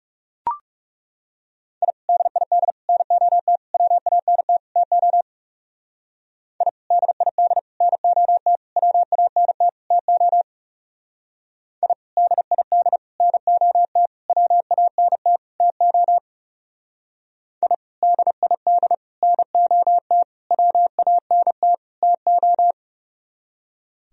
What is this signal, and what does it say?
Morse code, heard as a keyed tone, about 36 wpm, 700 Hz. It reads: I DID NOT WANT TO I DID NOT WANT TO I DID NOT WANT TO I DID NOT WANT TO
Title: Speed-Racing - Sets of 5 Words 30wpm